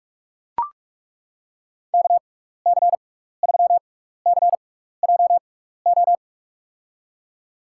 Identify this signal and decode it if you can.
Morse code, heard as a keyed tone, about 42 wpm, spaced as for 15 wpm, 700 Hz. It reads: KC3CJO
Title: USA Call Signs - 6 Characters - 6x Character Spacing - 45wpm